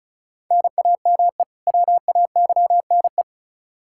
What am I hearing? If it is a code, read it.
Morse code, heard as a keyed tone, about 36 wpm, 700 Hz. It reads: NAME WAYNE
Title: Practice QSO - Exchange 1 - Beginning - 35wpm